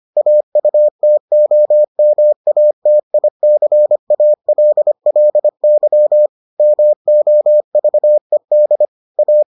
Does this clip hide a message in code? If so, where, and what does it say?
Yes: Morse code from 0.2 to 9.4 s, heard as a keyed tone, about 25 wpm, 600 Hz. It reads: AUTOMATICALLY MOVED A